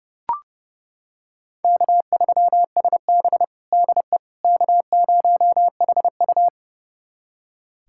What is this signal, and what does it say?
Morse code, heard as a keyed tone, about 30 wpm, 700 Hz. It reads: K3SB DE K0HU
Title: USA Callsign DE Callsign - 4 Characters - 30wpm